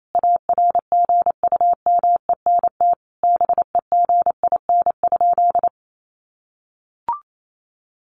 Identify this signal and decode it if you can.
Morse code, heard as a keyed tone, about 28 wpm, 700 Hz. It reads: ARGUMENT BEGIN?
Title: Sentences from Top 1400 Words 28wpm